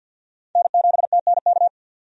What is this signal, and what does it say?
Morse code, heard as a keyed tone, about 47 wpm, 700 Hz. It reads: N7TNK
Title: USA Call Signs - 5 Characters - 50wpm